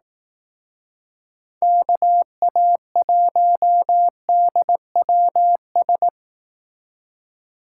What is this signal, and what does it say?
Morse code, heard as a keyed tone, about 19 wpm, 700 Hz. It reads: KA1DWS